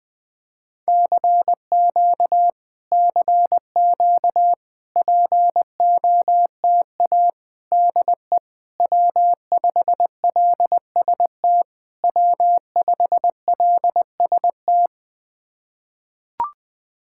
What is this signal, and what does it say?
Morse code, heard as a keyed tone, about 20 wpm, 700 Hz. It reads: CQ CQ POTA DE W5LST W5LST
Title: POTA QSO Practice - 20wpm